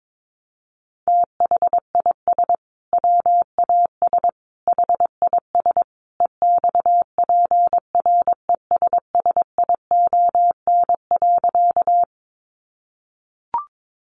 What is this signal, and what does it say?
Morse code, heard as a keyed tone, about 22 wpm, 700 Hz. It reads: THIS WAS HIS EXPRESSION.
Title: Sentences from Top 1200 Words 22wpm